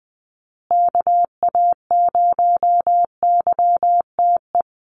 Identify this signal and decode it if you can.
Morse code, heard as a keyed tone, about 20 wpm, 700 Hz. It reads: KA0YTE
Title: USA Call Signs Repeated 2x 20wpm